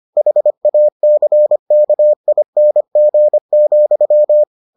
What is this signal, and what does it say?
Morse code, heard as a keyed tone, about 25 wpm, 600 Hz. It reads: HACKING,